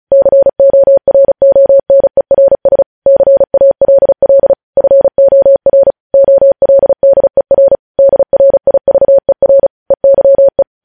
Morse code, heard as a keyed tone, about 36 wpm, 550 Hz. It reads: CORONERS CALL FOR OLDER DRIVER EYE
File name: podcast / News headlines-1x-35x35